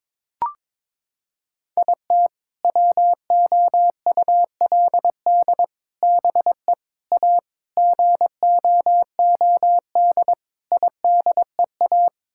Morse code, heard as a keyed tone, about 22 wpm, 700 Hz. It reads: IT WOULD BE A GOOD IDEA
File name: practice / 22wpm / Sets of 6 Words - 22wpm